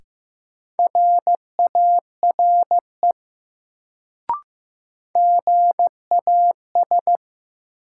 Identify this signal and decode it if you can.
Morse code, heard as a keyed tone, about 15 wpm, 700 Hz. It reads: RARE GAS
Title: Sentences from Top 1100 Words - New Word Drill Alt - 15wpm